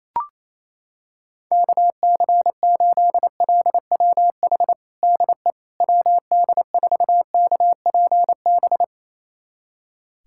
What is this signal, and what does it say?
Morse code, heard as a keyed tone, about 28 wpm, 700 Hz. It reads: KC8LWH DE WD4KPB
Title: USA Callsign DE Callsign 28wpm